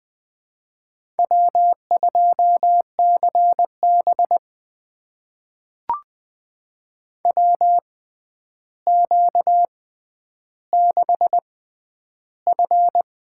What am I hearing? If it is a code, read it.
Morse code, heard as a keyed tone, about 20 wpm, spaced as for 10 wpm, 700 Hz. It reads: W2CB WQ6F
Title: USA Call Signs - 4 Characters - 6x Character Spacing - 20wpm